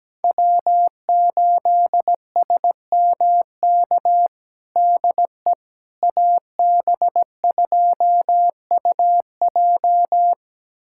Morse code, heard as a keyed tone, about 17 wpm, 700 Hz. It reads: W8SMK DE AB2UJ
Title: USA Callsign DE Callsign - 5 Characters - 17wpm